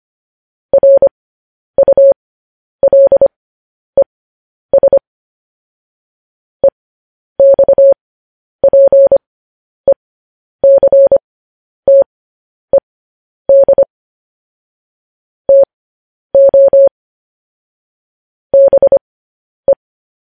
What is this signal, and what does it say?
Morse code, heard as a keyed tone, about 25 wpm, spaced as for 10 wpm, 550 Hz. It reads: RULES EXPECTED TO BE
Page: news-headlines-in-morse-code-at-10-wpm